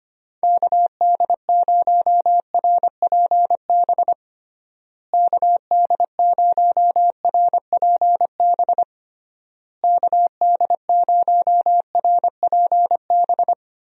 Morse code, heard as a keyed tone, about 25 wpm, 700 Hz. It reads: KD0RPB KD0RPB KD0RPB
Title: USA Call Signs - 6 Characters - Repeated 3x - 25wpm